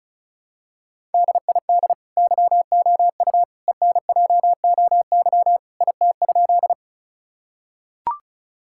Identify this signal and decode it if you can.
Morse code, heard as a keyed tone, about 36 wpm, 700 Hz. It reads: DID YOU ENJOY IT?
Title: Sentences from Top 1000 Words - Review Part 5 - 35wpm